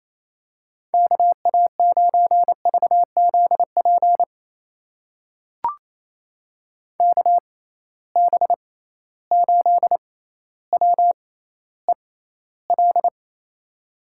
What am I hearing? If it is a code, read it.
Morse code, heard as a keyed tone, about 28 wpm, spaced as for 14 wpm, 700 Hz. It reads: KA9VZP KB8WEL
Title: USA Call Signs - 6 Characters - 6x Character Spacing - 28wpm